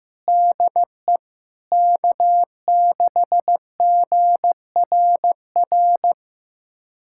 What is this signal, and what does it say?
Morse code, heard as a keyed tone, about 15 wpm, 700 Hz. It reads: DE K6GRR
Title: USA Callsign DE Callsign - 5 Characters - 15wpm